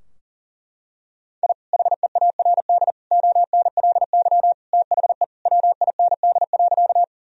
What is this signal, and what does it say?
Morse code, heard as a keyed tone, about 40 wpm, 700 Hz. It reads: I HEARD ONLY THE WIND.
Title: Sentences from Top 1000 Words - Review Part 2 - 40wpm